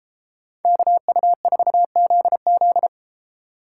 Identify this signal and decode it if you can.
Morse code, heard as a keyed tone, about 31 wpm, 700 Hz. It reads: KU4ZZ